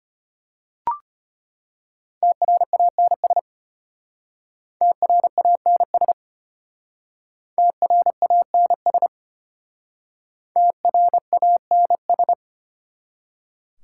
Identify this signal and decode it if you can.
Morse code, heard as a keyed tone, about 31 wpm, 700 Hz. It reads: TRANS TRANS TRANS TRANS
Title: Speed-Racing Prefixes and Suffixes 25wpm